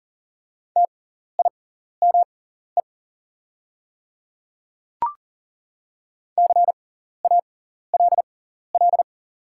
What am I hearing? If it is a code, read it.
Morse code, heard as a keyed tone, about 40 wpm, spaced as for 14 wpm, 700 Hz. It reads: TIME CALL
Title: 6x Character Spacing - Top 200 Words - 40wpm